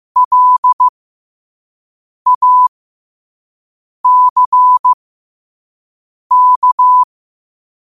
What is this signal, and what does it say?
Morse code, heard as a keyed tone, about 15 wpm, spaced as for 5 wpm, 1000 Hz. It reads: LACK
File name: qotdfiles / 5wpm